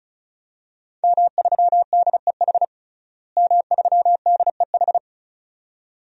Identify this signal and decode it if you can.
Morse code, heard as a keyed tone, about 36 wpm, 700 Hz. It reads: M3DEH M3DEH
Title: DXCC-100 Call Signs - 5 Characters - Repeated 2x - 35wpm